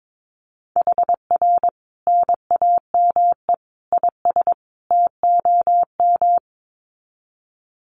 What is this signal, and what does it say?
Morse code, heard as a keyed tone, about 22 wpm, 700 Hz. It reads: HR NAME IS TOM
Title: Common QSO Phrases 22wpm